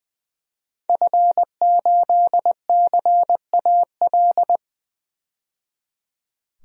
Morse code, heard as a keyed tone, about 21 wpm, 700 Hz. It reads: F8CAL
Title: DXCC-100 Call Signs - 5 Characters - 20wpm